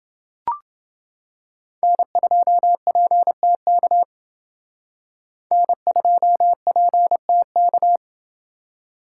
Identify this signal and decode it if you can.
Morse code, heard as a keyed tone, about 28 wpm, 700 Hz. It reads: N2PTK N2PTK